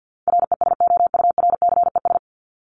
Morse code, heard as a keyed tone, about 50 wpm, 700 Hz. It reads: RESOURCES
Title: ES:S Suffix - Top 100 Words with Base Words 50wpm